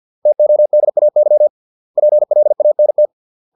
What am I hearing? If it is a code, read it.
Morse code, heard as a keyed tone, about 50 wpm, 600 Hz. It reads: TODAY PLANT